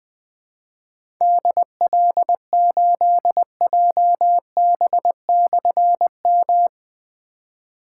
Morse code, heard as a keyed tone, about 20 wpm, 700 Hz. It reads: DL8JB/M
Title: Difficult Call Signs 20wpm